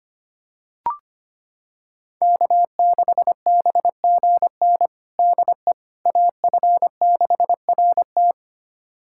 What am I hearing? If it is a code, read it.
Morse code, heard as a keyed tone, about 25 wpm, 700 Hz. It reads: K6BGN DE AF6RT
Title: USA Callsign DE Callsign - 5 Characters - 25wpm